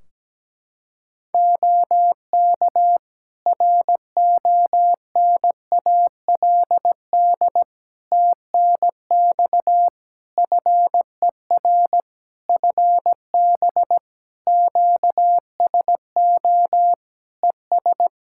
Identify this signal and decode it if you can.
Morse code, heard as a keyed tone, about 17 wpm, 700 Hz. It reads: OK RONALD TNX FER FB QSO ES
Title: Practice QSO - Exchange 4 - Ending - 17wpm